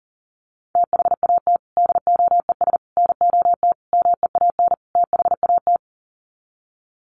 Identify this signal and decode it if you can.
Morse code, heard as a keyed tone, about 40 wpm, 700 Hz. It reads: THAT DOES NOT MEAN THAT
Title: Sets of 5 Words Encore 40wpm